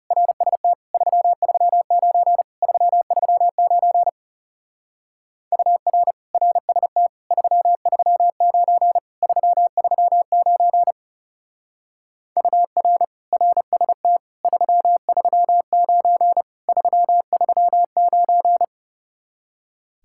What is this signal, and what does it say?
Morse code, heard as a keyed tone, about 36 wpm, 700 Hz. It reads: RST 339 339 UR RST 339 339 UR RST 339 339
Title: Speed-Racing Common QSO Phrases - 30wpm